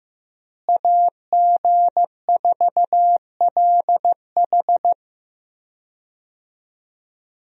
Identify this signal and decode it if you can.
Morse code, heard as a keyed tone, about 15 wpm, 700 Hz. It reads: AG4LH
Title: USA Call Signs - 5 Characters - Repeated 3x - 15wpm